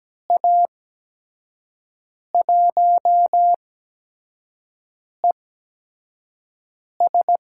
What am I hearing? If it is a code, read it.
Morse code, heard as a keyed tone, about 17 wpm, spaced as for 5 wpm, 700 Hz. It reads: A1ES
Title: USA Call Signs - 6 Characters - 8x Character Spacing - 17wpm